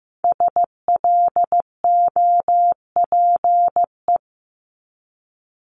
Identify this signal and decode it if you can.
Morse code, heard as a keyed tone, about 15 wpm, 700 Hz. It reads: SLOPE
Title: Multi-letter Phonemes Repeated 3x 15wpm